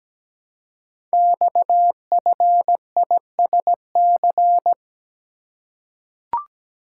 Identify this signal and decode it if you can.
Morse code, heard as a keyed tone, about 17 wpm, 700 Hz. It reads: XFISC
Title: Code Groups - 5 Letters - 17wpm